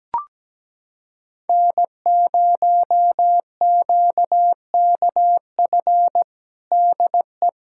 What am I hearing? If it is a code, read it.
Morse code, heard as a keyed tone, about 17 wpm, 700 Hz. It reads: N0QKF DE
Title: USA Callsign DE Callsign - 5 Characters - 17wpm